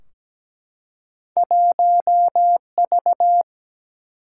Morse code, heard as a keyed tone, about 17 wpm, 700 Hz. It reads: 1V
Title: Single-Letter Single-Number 17wpm